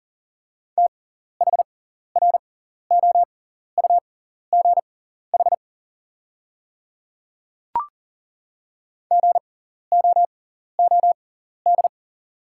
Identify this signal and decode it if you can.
Morse code, heard as a keyed tone, about 40 wpm, spaced as for 14 wpm, 690 Hz. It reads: THROUGH GOOD
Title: 6x Character Spacing - Top 200 Words - 40wpm